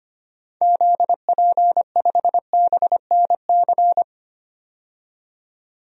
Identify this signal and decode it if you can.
Morse code, heard as a keyed tone, about 25 wpm, 700 Hz. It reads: ZP5BNC
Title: DXCC-100 Call Signs - 6 Characters - 25wpm